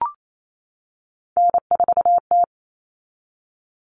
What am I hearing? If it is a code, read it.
Morse code, heard as a keyed tone, about 28 wpm, 700 Hz. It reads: N4T
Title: All Cut Numbers - 3-Digit 28wpm